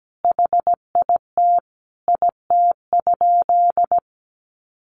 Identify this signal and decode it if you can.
Morse code, heard as a keyed tone, about 17 wpm, 700 Hz. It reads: HIT IT?
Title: Sentences from Top 1200 Words 17wpm